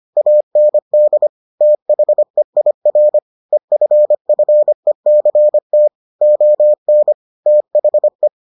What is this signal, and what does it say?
Morse code, heard as a keyed tone, about 25 wpm, 600 Hz. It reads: AND THEIR EFFECT ON THE